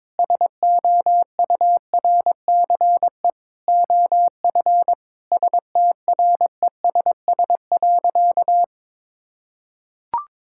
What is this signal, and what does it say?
Morse code, heard as a keyed tone, about 22 wpm, 700 Hz. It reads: SOURCE OF STRESS.
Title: Sentences from Top 1000 Words - Review Part 6 - 22wpm